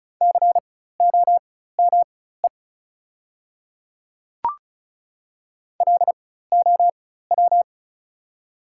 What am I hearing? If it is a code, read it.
Morse code, heard as a keyed tone, about 34 wpm, spaced as for 16 wpm, 700 Hz. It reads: COME LOW
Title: 4x Character Spacing - Top 200 Words - 35wpm